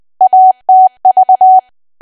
Morse code, heard as a keyed tone, about 20 wpm, 750 Hz. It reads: ATV
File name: Archive / 20 WPM / 20WPM